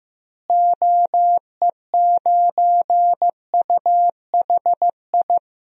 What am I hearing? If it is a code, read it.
Morse code, heard as a keyed tone, about 15 wpm, 700 Hz. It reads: OE9UHI
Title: DXCC-100 Call Signs - 6 Characters - Repeated 2x - 15wpm